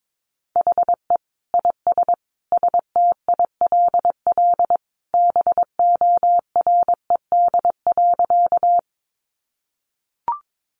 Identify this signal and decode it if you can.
Morse code, heard as a keyed tone, about 22 wpm, 700 Hz. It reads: HE IS STILL BORED.
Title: Easy Phrases 22wpm